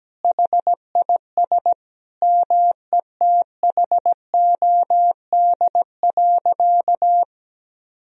Morse code, heard as a keyed tone, about 17 wpm, 700 Hz. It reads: HIS METHOD.
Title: Sentences from Top 1000 Words - Review Part 3 - 17wpm